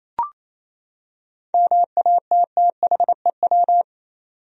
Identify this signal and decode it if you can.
Morse code, heard as a keyed tone, about 28 wpm, 700 Hz. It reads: MATTHEW
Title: USA Common Names - 28wpm